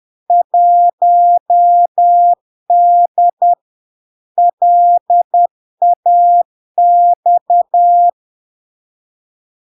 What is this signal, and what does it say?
Morse code, heard as a keyed tone, about 10 wpm, 700 Hz. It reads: 1D LAX
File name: LARK-FD-2022-Log-in-CW-10wpm-700-Hz